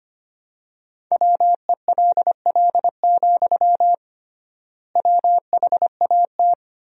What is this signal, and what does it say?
Morse code, heard as a keyed tone, about 25 wpm, 700 Hz. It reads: WELL, WHAT
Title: 3x Word-Spacing - Sentences from Top 500 Words - 25wpm